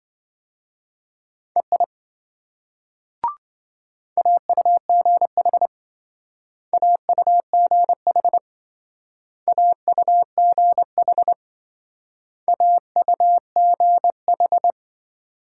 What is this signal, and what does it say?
Morse code, heard as a keyed tone, about 27 wpm, 700 Hz. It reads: EI AUGH AUGH AUGH AUGH